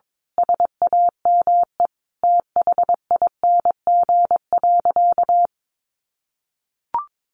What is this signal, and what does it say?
Morse code, heard as a keyed tone, about 22 wpm, 700 Hz. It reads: SAME THING.
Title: Sentences from Top 1100 Words 22wpm